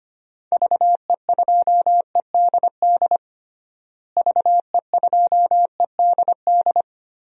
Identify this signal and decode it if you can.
Morse code, heard as a keyed tone, about 25 wpm, 700 Hz. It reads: VE2EDD VE2EDD
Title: DXCC-100 Call Signs - 6 Characters - Repeated 2x - 25wpm